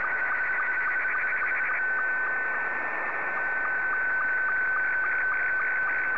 SSTV (slow-scan television), Scottie 2.